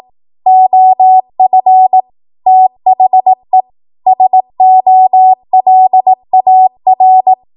Morse code, heard as a keyed tone, about 18 wpm, 750 Hz. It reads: OF THE SOLAR
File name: Morse / 18WPM